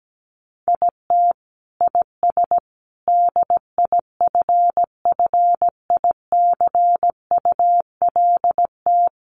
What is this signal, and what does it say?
Morse code, heard as a keyed tone, about 17 wpm, 700 Hz. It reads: IT IS DIFFICULT
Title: Sentences from Top 1100 Words 17wpm